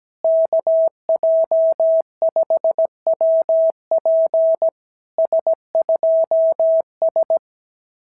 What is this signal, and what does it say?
Morse code, heard as a keyed tone, about 17 wpm, 660 Hz. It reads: KJ5WP S2S
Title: SOTA QSO Practice 17wpm